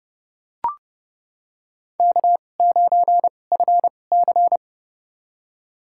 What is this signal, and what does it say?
Morse code, heard as a keyed tone, about 29 wpm, spaced as for 21 wpm, 700 Hz. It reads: K9FC
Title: USA Call Signs - 2x Character Spacing - 30wpm